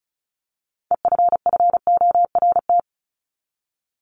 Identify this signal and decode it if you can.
Morse code, heard as a keyed tone, about 34 wpm, 700 Hz. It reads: EFFORT
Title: Top 1001-1250 Words 35wpm